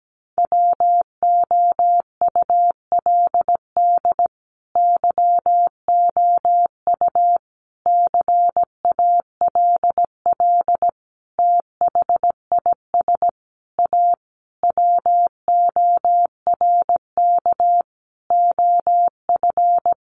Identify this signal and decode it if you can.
Morse code, heard as a keyed tone, about 17 wpm, 700 Hz. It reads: WOULD YOU CALL THIS A WORK OF
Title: Sentences from Top 600 Words 17wpm